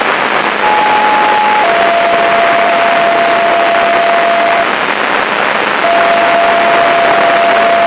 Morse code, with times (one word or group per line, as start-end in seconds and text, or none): none